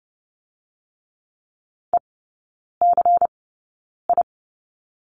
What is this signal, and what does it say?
Morse code, heard as a keyed tone, about 30 wpm, spaced as for 9 wpm, 690 Hz. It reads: ECI
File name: Lesson 16e - Letter c - Three Characters 30wpm